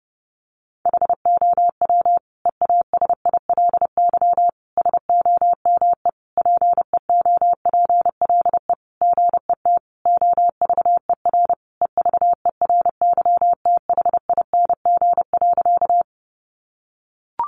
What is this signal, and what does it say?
Morse code, heard as a keyed tone, about 30 wpm, 700 Hz. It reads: HOW EASILY SOME PEOPLE GET OVER EVERYTHING.
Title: Sentences from Top 900 Words 30wpm